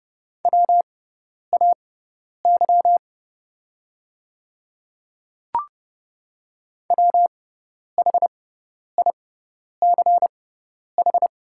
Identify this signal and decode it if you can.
Morse code, heard as a keyed tone, about 30 wpm, spaced as for 10 wpm, 700 Hz. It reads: WAY WHICH
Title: 6x Character Spacing - Top 100 Words - 30wpm